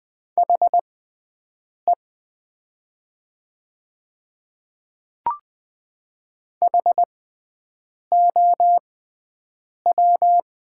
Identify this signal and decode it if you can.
Morse code, heard as a keyed tone, about 20 wpm, spaced as for 7 wpm, 700 Hz. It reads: HE HOW
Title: 6x Character Spacing - Top 100 Words - 20wpm